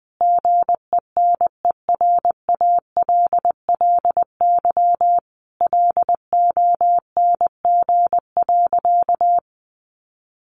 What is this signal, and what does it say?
Morse code, heard as a keyed tone, about 20 wpm, 700 Hz. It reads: GENERALLY LONG.
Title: Sentences from Top 600 Words 20wpm